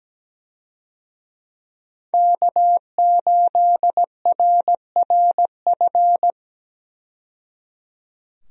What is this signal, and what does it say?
Morse code, heard as a keyed tone, about 17 wpm, 700 Hz. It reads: K8RRF
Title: USA Call Signs - 5 Characters - 17wpm